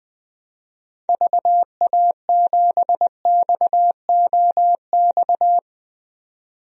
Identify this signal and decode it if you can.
Morse code, heard as a keyed tone, about 20 wpm, 700 Hz. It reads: VA7XOX
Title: DXCC-100 Call Signs - 6 Characters - 20wpm